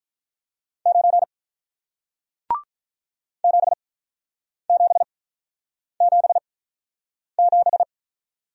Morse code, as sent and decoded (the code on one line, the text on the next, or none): ----. / --... --... --... --...
9 7777